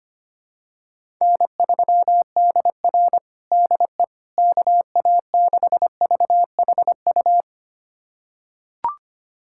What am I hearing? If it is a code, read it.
Morse code, heard as a keyed tone, about 25 wpm, 700 Hz. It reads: N3DR DE KA6VHU
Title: USA Callsign DE Callsign 25wpm